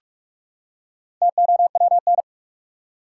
Morse code, heard as a keyed tone, about 42 wpm, 700 Hz. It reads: TOWN
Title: Four Letter Words - 45wpm